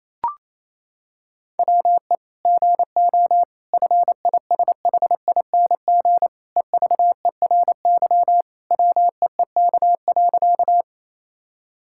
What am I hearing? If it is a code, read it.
Morse code, heard as a keyed tone, about 28 wpm, 700 Hz. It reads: WE GO FISHING EVERY WEEK.
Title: Sentences from Top 1000 Words - Review Part 5 - 28wpm